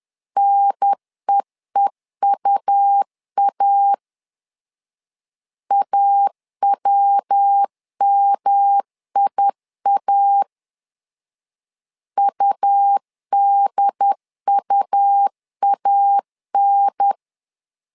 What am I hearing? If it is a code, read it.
Morse code, heard as a keyed tone, about 10 wpm, 800 Hz. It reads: NEEUA AWMIA UDUAN
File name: LEZIONI / 3 MWRO / 3 MW 10wpm 1959